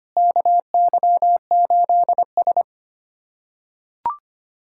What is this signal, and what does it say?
Morse code, heard as a keyed tone, about 25 wpm, 700 Hz. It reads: KY8S
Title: USA Callsign DE Callsign - 4 Characters - 25wpm